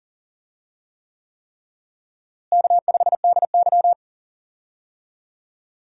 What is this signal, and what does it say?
Morse code, heard as a keyed tone, about 40 wpm, 700 Hz. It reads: K5DY
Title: USA Call Signs - 4 Characters - Repeated 3x - 40wpm